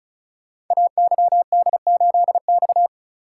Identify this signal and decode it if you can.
Morse code, heard as a keyed tone, about 36 wpm, 700 Hz. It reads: AYD8X
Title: Code Groups - 5 Letter:Numbers - 35wpm